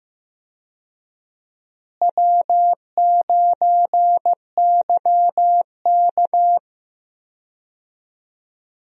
Morse code, heard as a keyed tone, about 15 wpm, 700 Hz. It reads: W9YK
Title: USA Call Signs - 4 Characters - Repeated 3x - 15wpm